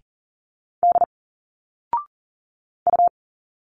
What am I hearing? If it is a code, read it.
Morse code, heard as a keyed tone, about 40 wpm, 700 Hz. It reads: D U
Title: Single Letters 40wpm